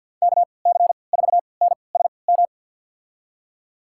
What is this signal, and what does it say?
Morse code, heard as a keyed tone, about 49 wpm, spaced as for 28 wpm, 690 Hz. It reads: KC4NSM
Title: USA Call Signs - 6 Characters - 3x Character Spacing - 50wpm